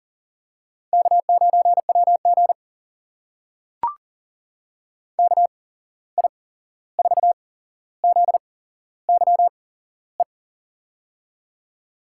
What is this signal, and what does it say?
Morse code, heard as a keyed tone, about 42 wpm, spaced as for 15 wpm, 700 Hz. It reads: K9WG KI4ZYE
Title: USA Call Signs - 8x Character Spacing - 40wpm